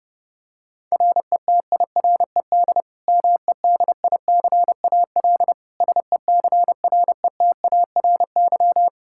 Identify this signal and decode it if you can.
Morse code, heard as a keyed tone, about 32 wpm, 700 Hz. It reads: RETIRED MEDICAL SECRETARY
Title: Practice QSO - Exchange 3 - More Info - 30wpm